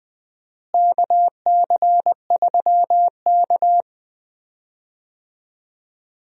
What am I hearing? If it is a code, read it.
Morse code, heard as a keyed tone, about 20 wpm, 700 Hz. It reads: KC3K
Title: USA Call Signs - 4 Characters - Repeated 2x - 20wpm